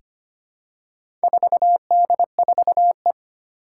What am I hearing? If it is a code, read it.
Morse code, heard as a keyed tone, about 25 wpm, 700 Hz. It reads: 4D4E
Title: Four Challenging Letter-Number Combinations 25wpm